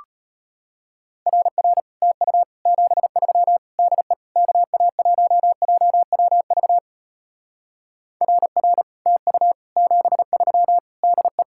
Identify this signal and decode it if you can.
Morse code, heard as a keyed tone, about 32 wpm, 700 Hz. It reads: RR TU 73 DE KA1JWV RR TU 73 DE